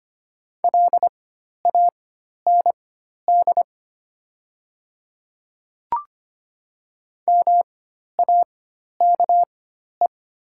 Morse code, heard as a keyed tone, about 25 wpm, spaced as for 12 wpm, 690 Hz. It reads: LAND MAKE
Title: 4x Character Spacing - Top 300 Words - 25wpm